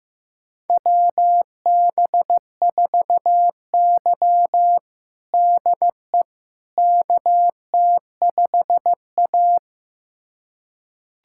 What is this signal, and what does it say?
Morse code, heard as a keyed tone, about 16 wpm, 700 Hz. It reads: WB4Y DE KT5A
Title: USA Callsign DE Callsign - 4 Characters - 15wpm